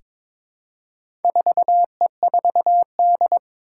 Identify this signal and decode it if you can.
Morse code, heard as a keyed tone, about 22 wpm, 700 Hz. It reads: 4E4D
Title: Four Challenging Letter-Number Combinations 22wpm